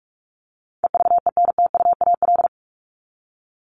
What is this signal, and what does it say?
Morse code, heard as a keyed tone, about 44 wpm, 700 Hz. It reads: EVENTUAL
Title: LY Suffix - Top 100 Words with Base Words 45wpm